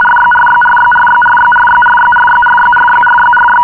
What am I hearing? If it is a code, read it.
RTTY, about 50 baud. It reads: RYRYRYRYRYRYRYRYRYRYRYRY
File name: agerpres bucharest-rtty-50 bd